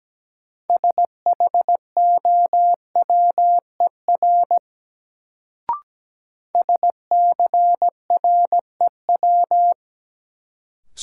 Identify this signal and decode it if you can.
Morse code, heard as a keyed tone, about 17 wpm, 700 Hz. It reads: SHOWER SCREW
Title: Sentences from Top 1600 Words - New Word Drill - 17wpm